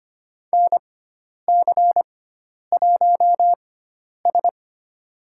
Morse code, heard as a keyed tone, about 25 wpm, spaced as for 10 wpm, 700 Hz. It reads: NC1S
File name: Lesson 16g - Letter c - Call Signs 25wpm